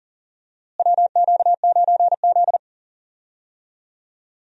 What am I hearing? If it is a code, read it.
Morse code, heard as a keyed tone, about 40 wpm, 700 Hz. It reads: WQ9Z
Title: USA Call Signs - 4 Characters - Repeated 3x - 40wpm